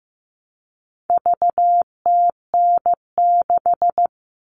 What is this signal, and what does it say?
Morse code, heard as a keyed tone, about 15 wpm, 700 Hz. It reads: VTN6
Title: All Cut Numbers - 4-Digit 15wpm